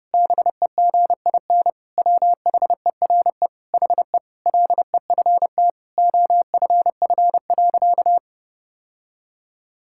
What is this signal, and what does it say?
Morse code, heard as a keyed tone, about 30 wpm, 700 Hz. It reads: BEGIN WHERE HE LEFT OFF.
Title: Sentences from Top 1000 Words - Review Part 6 - 30wpm